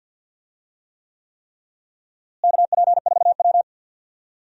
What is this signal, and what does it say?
Morse code, heard as a keyed tone, about 48 wpm, 700 Hz. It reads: KP4W
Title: DXCC-100 Call Signs - 4 Characters - Repeated 3x - 50wpm